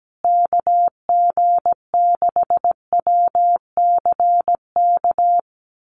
Morse code, heard as a keyed tone, about 17 wpm, 700 Hz. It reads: KG6WCK
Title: USA Call Signs Repeated 3x 17wpm